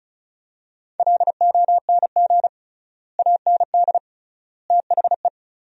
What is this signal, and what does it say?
Morse code, heard as a keyed tone, about 36 wpm, 700 Hz. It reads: LONG AND THE
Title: 3x Word-Spacing - Sentences from Top 400 Words - 35wpm